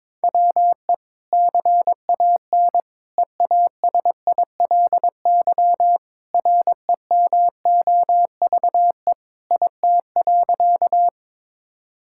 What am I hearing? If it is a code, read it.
Morse code, heard as a keyed tone, about 22 wpm, 700 Hz. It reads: WE CAN EASILY REMOVE IT.
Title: Sentences from Top 1000 Words - Review Part 5 - 22wpm